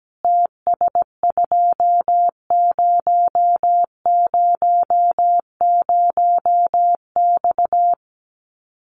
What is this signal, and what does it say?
Morse code, heard as a keyed tone, about 17 wpm, 700 Hz. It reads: TS2000X
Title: Common QSO Phrases 17wpm